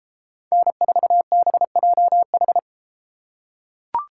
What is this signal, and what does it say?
Morse code, heard as a keyed tone, about 33 wpm, 700 Hz. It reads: N4BJH